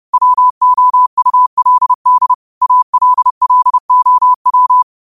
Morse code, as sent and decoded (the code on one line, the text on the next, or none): .-- --- ..- .-.. -.. / .- .-.. .-.. --- .--
WOULD ALLOW